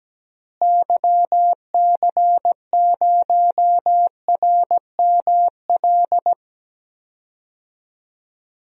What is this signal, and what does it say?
Morse code, heard as a keyed tone, about 17 wpm, 700 Hz. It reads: YC0RML
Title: DXCC-100 Call Signs - 6 Characters - Repeated 3x - 17wpm